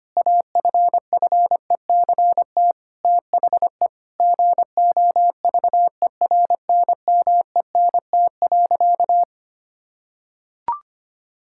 Morse code, heard as a keyed tone, about 25 wpm, 700 Hz. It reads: AFFECT THE GOVERNMENT.
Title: Sentences from Top 1000 Words - Review Part 6 - 25wpm